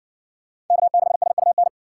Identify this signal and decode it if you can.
Morse code, heard as a keyed tone, about 57 wpm, 700 Hz. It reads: K6SUN